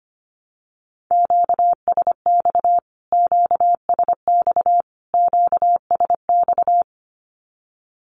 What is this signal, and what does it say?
Morse code, heard as a keyed tone, about 25 wpm, 700 Hz. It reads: QSX QSX QSX
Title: Q-Codes Repeated 3x 25wpm